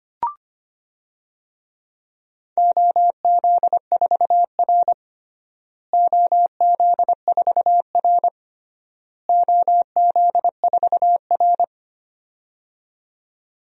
Morse code, heard as a keyed tone, about 25 wpm, 700 Hz. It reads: OZ4R OZ4R OZ4R
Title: DXCC-100 Call Signs - 4 Characters - Repeated 3x - 25wpm